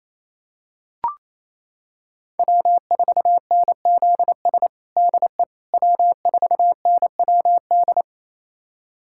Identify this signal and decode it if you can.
Morse code, heard as a keyed tone, about 27 wpm, 700 Hz. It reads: W4NZS DE W4NWD
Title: USA Callsign DE Callsign - 5 Characters - 28wpm